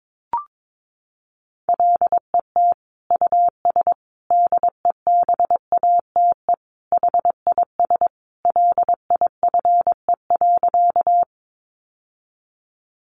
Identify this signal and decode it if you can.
Morse code, heard as a keyed tone, about 22 wpm, 700 Hz. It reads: LET US DEBATE HIS LIFE.
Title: Sentences from Top 1100 Words 22wpm